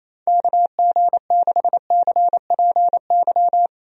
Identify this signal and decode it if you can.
Morse code, heard as a keyed tone, about 28 wpm, 700 Hz. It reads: KG6CPY
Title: USA Call Signs - 6 Characters - 28wpm